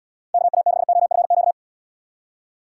Morse code, heard as a keyed tone, about 79 wpm, 700 Hz. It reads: KA6OFQ